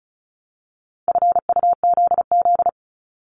Morse code, heard as a keyed tone, about 36 wpm, 700 Hz. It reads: FUZZ
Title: Multi-letter Phonemes Repeated 3x 35wpm